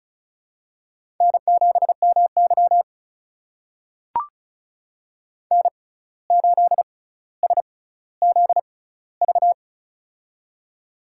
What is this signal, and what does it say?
Morse code, heard as a keyed tone, about 36 wpm, spaced as for 17 wpm, 700 Hz. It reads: N7MY N8SZV
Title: USA Call Signs - 6x Character Spacing - 35wpm